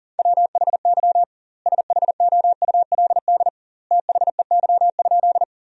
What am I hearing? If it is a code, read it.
Morse code, heard as a keyed tone, about 40 wpm, 700 Hz. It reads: WHY SHOULD THEY?
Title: 2x Word-Spacing - Sentences from Top 300 Words - 40wpm